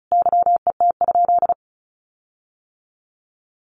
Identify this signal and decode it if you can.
Morse code, heard as a keyed tone, about 34 wpm, 700 Hz. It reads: YET?
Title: Sentences from Top 1500 Words 35wpm